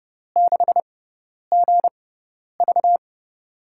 Morse code, heard as a keyed tone, about 30 wpm, spaced as for 10 wpm, 700 Hz. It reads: 6GV
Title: USA Call Signs - 5 Characters - 6x Character Spacing - 30wpm